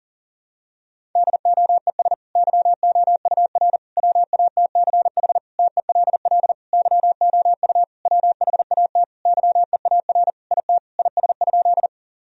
Morse code, heard as a keyed tone, about 40 wpm, 700 Hz. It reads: DOES YOUR WATCH TELL YOU WHAT YEAR IT IS?
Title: Sentences from Top 1000 Words - Review Part 1 - 40wpm